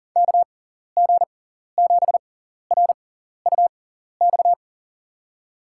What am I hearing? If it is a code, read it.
Morse code, heard as a keyed tone, about 40 wpm, spaced as for 14 wpm, 700 Hz. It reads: KG7RUX
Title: USA Call Signs - 6 Characters - 6x Character Spacing - 40wpm